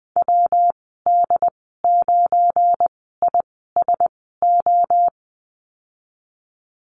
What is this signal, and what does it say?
Morse code, heard as a keyed tone, about 20 wpm, spaced as for 15 wpm, 700 Hz. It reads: WD9ISO
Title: 186 Lesson 30g - Letter b - Call Signs 20wpm